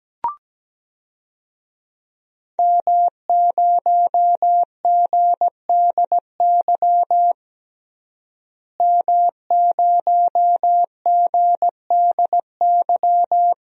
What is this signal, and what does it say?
Morse code, heard as a keyed tone, about 17 wpm, 700 Hz. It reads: M0GDY M0GDY
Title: DXCC-100 Call Signs - 5 Characters - Repeated 2x - 17wpm